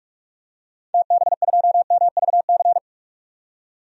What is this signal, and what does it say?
Morse code, heard as a keyed tone, about 41 wpm, 700 Hz. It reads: TB2MVC
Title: DXCC-100 Call Signs - 6 Characters - 45wpm